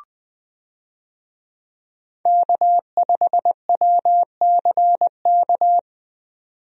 Morse code, heard as a keyed tone, about 20 wpm, 700 Hz. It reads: K5WCK
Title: USA Call Signs - 5 Characters - 20wpm